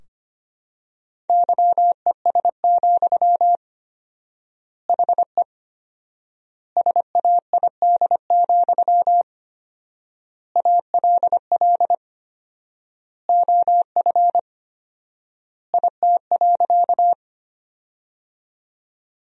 Morse code, heard as a keyed tone, about 25 wpm, 700 Hz. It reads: YES, HE SAID, ALL OF IT.
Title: 4x Word-Spacing - Sentences from Top 500 Words - 25wpm